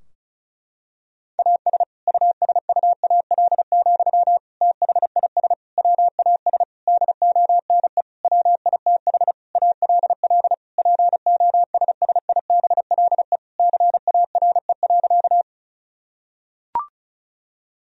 Morse code, heard as a keyed tone, about 34 wpm, 700 Hz. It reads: AS USUAL, THIS WAS DONE WITH ALL POSSIBLE CARE.
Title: Sentences from Top 1000 Words - Review Part 2 - 35wpm